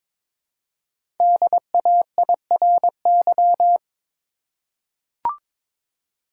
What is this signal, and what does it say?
Morse code, heard as a keyed tone, about 22 wpm, 700 Hz. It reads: DAIRY
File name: 5 Letter Words - First 3 Letters Given - 22wpm